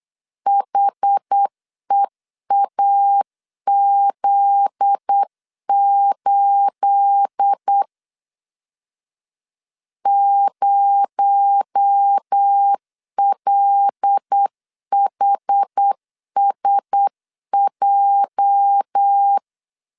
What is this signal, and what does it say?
Morse code, heard as a keyed tone, about 8 wpm, 800 Hz. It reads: HEAZ8 0LHSJ
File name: LEZIONI / 9 28 8wpm 3512